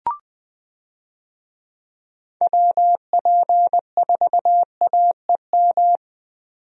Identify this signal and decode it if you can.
Morse code, heard as a keyed tone, about 20 wpm, 700 Hz. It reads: WP4AEM
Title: USA Call Signs - 6 Characters - 20wpm